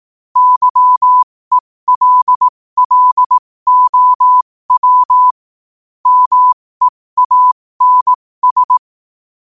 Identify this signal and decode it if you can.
Morse code, heard as a keyed tone, about 18 wpm, spaced as for 15 wpm, 1000 Hz. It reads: YELLOW MEANS
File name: qotdfiles / 15wpm